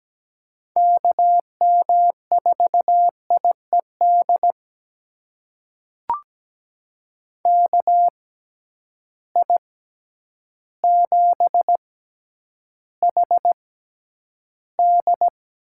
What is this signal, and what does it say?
Morse code, heard as a keyed tone, about 17 wpm, spaced as for 9 wpm, 700 Hz. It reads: KM4IED KI7HD
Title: USA Call Signs - 6 Characters - 6x Character Spacing - 17wpm